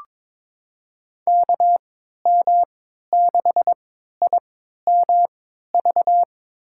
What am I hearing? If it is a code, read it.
Morse code, heard as a keyed tone, about 22 wpm, spaced as for 13 wpm, 700 Hz. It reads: KM6IMV
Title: USA Call Signs - 6 Characters - 3x Character Spacing - 22wpm